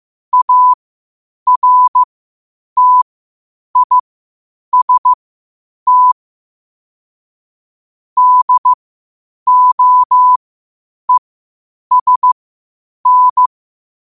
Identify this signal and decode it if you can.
Morse code, heard as a keyed tone, about 15 wpm, spaced as for 8 wpm, 1000 Hz. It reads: ARTIST DOESN
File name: qotdfiles / 8wpm